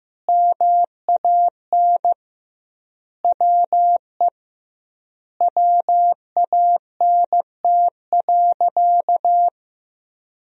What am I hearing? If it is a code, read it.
Morse code, heard as a keyed tone, about 15 wpm, 700 Hz. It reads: MAN WE WANT.
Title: 2x Word-Spacing - Sentences from Top 200 Words - 15wpm